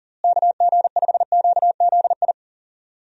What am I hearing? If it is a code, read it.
Morse code, heard as a keyed tone, about 40 wpm, 700 Hz. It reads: KG5QZI
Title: USA Call Signs - 6 Characters - Repeated 3x - 40wpm